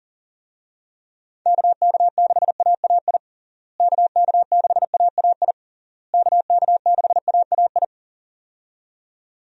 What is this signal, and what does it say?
Morse code, heard as a keyed tone, about 40 wpm, 700 Hz. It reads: KK6AAI KK6AAI KK6AAI
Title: USA Call Signs - 6 Characters - Repeated 3x - 40wpm